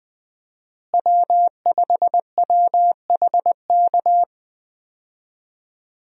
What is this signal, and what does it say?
Morse code, heard as a keyed tone, about 20 wpm, 700 Hz. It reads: W5WHK
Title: USA Call Signs - 5 Characters - Repeated 3x - 20wpm